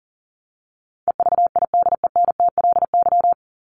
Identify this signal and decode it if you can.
Morse code, heard as a keyed tone, about 40 wpm, 700 Hz. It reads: EVIDENTLY
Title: LY Suffix - Top 100 Words with Base Words 40wpm